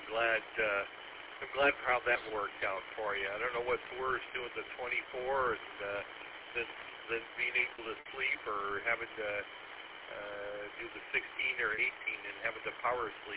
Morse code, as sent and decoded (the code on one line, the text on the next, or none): none